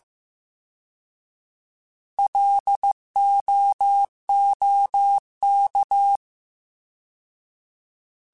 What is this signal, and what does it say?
Morse code, heard as a keyed tone, about 15 wpm, 800 Hz. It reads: LOOK
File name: Alphabetical / Alphabetical mp3s / Ls @13